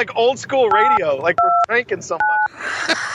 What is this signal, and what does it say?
DTMF, heard as telephone touch tones, tones of about 290 ms, with speech. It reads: 829